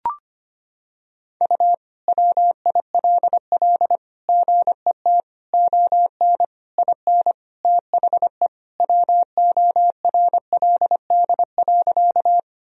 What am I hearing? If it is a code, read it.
Morse code, heard as a keyed tone, about 25 wpm, 700 Hz. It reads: U WILL GET ON IN THE WORLD.
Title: Sentences from Top 200 Words and Abbreviations 25wpm